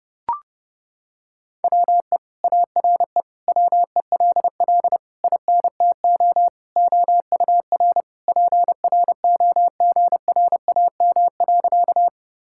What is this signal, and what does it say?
Morse code, heard as a keyed tone, about 32 wpm, 700 Hz. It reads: WE ARE WELL INTO OUR PROGRAM.
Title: Sentences from Top 1000 Words - Review Part 5 - 30wpm